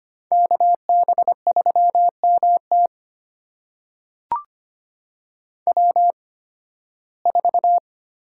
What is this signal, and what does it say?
Morse code, heard as a keyed tone, about 25 wpm, spaced as for 16 wpm, 700 Hz. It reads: KB3MT W4
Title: USA Call Signs - 5 Characters - 8x Character Spacing - 25wpm